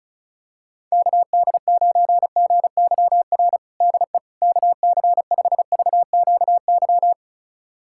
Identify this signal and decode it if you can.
Morse code, heard as a keyed tone, about 36 wpm, 700 Hz. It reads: KD9GYR DE KC5VQY
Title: USA Callsign DE Callsign - 6 Characters - 35wpm